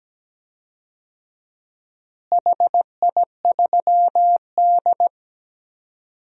Morse code, heard as a keyed tone, about 17 wpm, 700 Hz. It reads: HI3D
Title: DXCC-100 Call Signs - 4 Characters - Repeated 2x - 17wpm